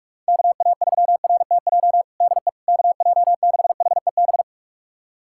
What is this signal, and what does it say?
Morse code, heard as a keyed tone, about 46 wpm, 700 Hz. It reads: KA3RTJ DE KJ6HEB
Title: USA Callsign DE Callsign 45wpm